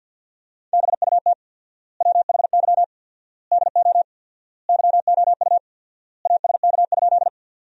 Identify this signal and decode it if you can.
Morse code, heard as a keyed tone, about 50 wpm, 700 Hz. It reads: BUT WHY DO YOU ASK?
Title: 4x Word-Spacing - Sentences from Top 200 Words - 50wpm